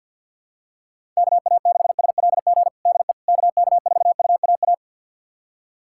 Morse code, heard as a keyed tone, about 46 wpm, 700 Hz. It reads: KA6SLG DE KK4UAA
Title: USA Callsign DE Callsign - 6 Characters - 50wpm